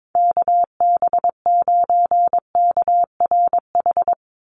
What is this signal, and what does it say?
Morse code, heard as a keyed tone, about 22 wpm, 700 Hz. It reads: KB9KRH
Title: USA Call Signs 22wpm